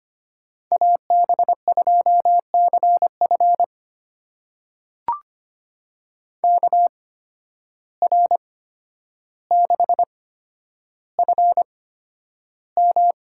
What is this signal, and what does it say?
Morse code, heard as a keyed tone, about 25 wpm, spaced as for 11 wpm, 700 Hz. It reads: AB2CF KR6FM
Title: USA Call Signs - 5 Characters - 8x Character Spacing - 25wpm